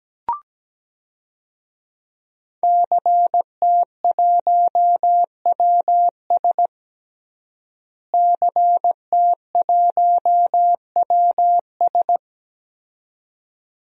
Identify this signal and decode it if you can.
Morse code, heard as a keyed tone, about 17 wpm, 700 Hz. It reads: CT1WS CT1WS
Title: DXCC-100 Call Signs - 5 Characters - Repeated 2x - 17wpm